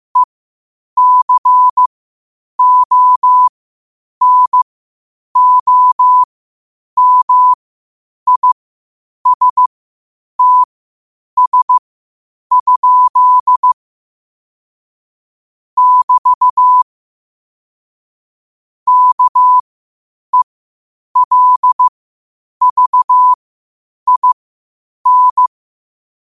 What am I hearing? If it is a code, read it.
Morse code, heard as a keyed tone, about 15 wpm, spaced as for 8 wpm, 1000 Hz. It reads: ECONOMISTS? <BT> KELVIN
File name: qotdfiles / 8wpm